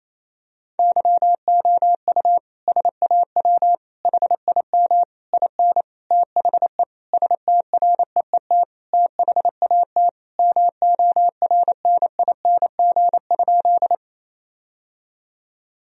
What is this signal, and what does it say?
Morse code, heard as a keyed tone, about 28 wpm, 700 Hz. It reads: YOU SAW HIM IN THE STREET THAT MORNING?
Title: Sentences from Top 1000 Words - Review Part 3 - 28wpm